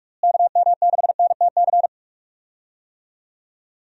Morse code, heard as a keyed tone, about 41 wpm, 700 Hz. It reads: KM6NTC
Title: USA Call Signs - 6 Characters - Repeated 3x - 45wpm